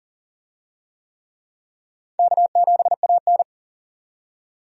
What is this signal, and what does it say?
Morse code, heard as a keyed tone, about 40 wpm, 700 Hz. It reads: K7AN
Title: USA Call Signs - 4 Characters - 40wpm